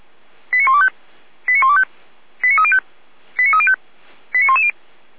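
Five-tone selective call (ZVEI), heard as quick runs of five tones, tones of about 70 ms. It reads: 89125 89125 89305 89305 89110
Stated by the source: ZVEI five-tone call received by SDR.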